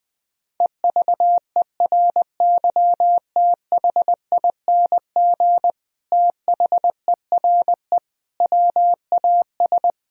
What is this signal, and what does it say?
Morse code, heard as a keyed tone, about 20 wpm, 700 Hz. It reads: EVERYTHING THERE WAS
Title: Sentences from Top 1000 Words - Review Part 2 - 20wpm